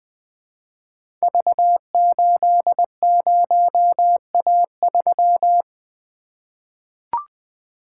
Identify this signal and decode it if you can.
Morse code, heard as a keyed tone, about 20 wpm, 700 Hz. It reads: V80A3
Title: Code Groups - 5 Letter:Numbers - 20wpm